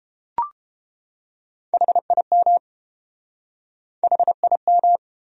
Morse code, heard as a keyed tone, about 29 wpm, 700 Hz. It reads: HIM HIM